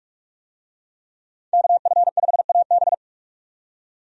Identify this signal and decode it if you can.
Morse code, heard as a keyed tone, about 47 wpm, 700 Hz. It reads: KF5AB
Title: USA Call Signs - 5 Characters - 45wpm